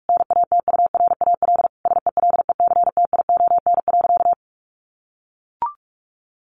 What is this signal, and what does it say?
Morse code, heard as a keyed tone, about 44 wpm, 700 Hz. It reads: NATURAL SELECTION.
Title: Sentences from Top 1400 Words 45wpm